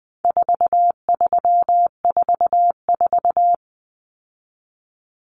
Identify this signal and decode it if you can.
Morse code, heard as a keyed tone, about 20 wpm, 700 Hz. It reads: 4344
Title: Common Cut Numbers - 4-Digit 20wpm